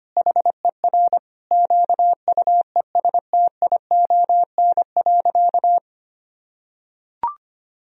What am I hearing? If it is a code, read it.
Morse code, heard as a keyed tone, about 25 wpm, 700 Hz. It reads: HER QUESTION.
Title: Sentences from Top 1000 Words - Review Part 2 - 25wpm